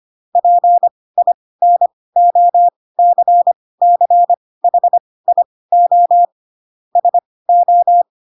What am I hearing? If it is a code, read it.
Morse code, heard as a keyed tone, about 25 wpm, spaced as for 18 wpm, 700 Hz. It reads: PINOCCHIO SO